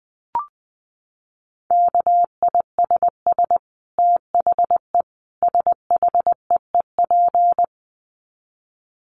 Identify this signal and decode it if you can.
Morse code, heard as a keyed tone, about 20 wpm, 700 Hz. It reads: KISS THE SHEEP
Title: Dit Challenge 20wpm